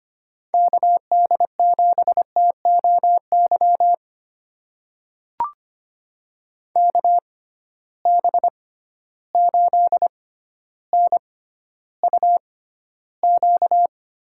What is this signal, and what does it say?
Morse code, heard as a keyed tone, about 25 wpm, spaced as for 13 wpm, 700 Hz. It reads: KD7TOY KB8NUQ
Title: USA Call Signs - 6 Characters - 6x Character Spacing - 25wpm